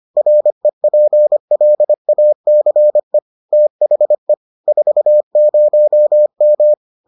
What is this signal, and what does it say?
Morse code, heard as a keyed tone, about 25 wpm, 600 Hz. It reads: REPLACE THE 40M